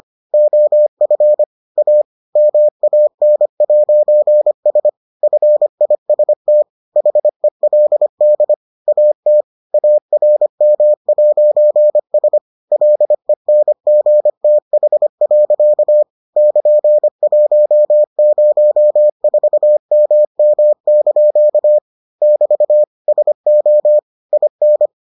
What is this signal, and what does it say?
Morse code, heard as a keyed tone, about 25 wpm, 600 Hz. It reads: OF A MAN'S FIST HELD AT ARM'S LENGTH. <KN>104MM) <BT> SO IN